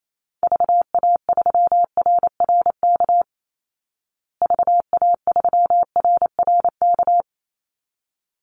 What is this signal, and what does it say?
Morse code, heard as a keyed tone, about 28 wpm, 700 Hz. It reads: VA3RRK VA3RRK
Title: Call Signs Repeated 3x - 28wpm